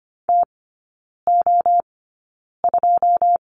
Morse code, heard as a keyed tone, about 25 wpm, spaced as for 9 wpm, 700 Hz. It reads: TO2
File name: course / 25:10wpm / 093 Lesson 16e - Letter c - Three Characters 25wpm